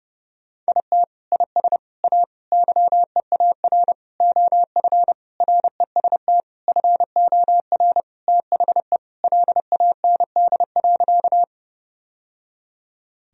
Morse code, heard as a keyed tone, about 32 wpm, 700 Hz. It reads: IT IS A YEAR OF REST FOR THE LAND.
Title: Sentences from Top 1000 Words - Review Part 1 - 30wpm